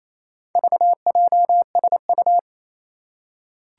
Morse code, heard as a keyed tone, about 28 wpm, 700 Hz. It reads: VJSU
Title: Four Challenging Letter-Number Combinations 28wpm